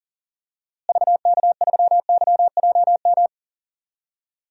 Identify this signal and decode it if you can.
Morse code, heard as a keyed tone, about 40 wpm, 700 Hz. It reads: VK3YJM
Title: DXCC-100 Call Signs - 6 Characters - Repeated 2x - 40wpm